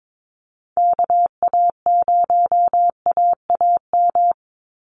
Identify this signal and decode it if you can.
Morse code, heard as a keyed tone, about 22 wpm, 690 Hz. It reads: KA0AAM
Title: Call Signs Repeated 3x - 22wpm